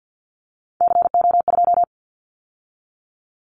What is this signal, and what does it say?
Morse code, heard as a keyed tone, about 50 wpm, 700 Hz. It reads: CO2
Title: Callsign Prefixes for Top 100 DXCC - 50wpm